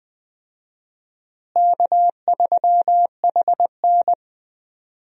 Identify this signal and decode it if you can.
Morse code, heard as a keyed tone, about 21 wpm, 700 Hz. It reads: K3HN
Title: USA Call Signs - 4 Characters - Repeated 3x - 20wpm